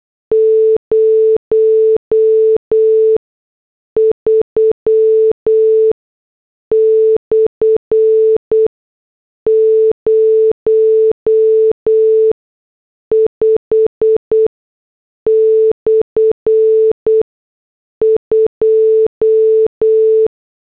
Morse code, heard as a keyed tone, about 8 wpm, spaced as for 6 wpm, 440 Hz. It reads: 03/05/2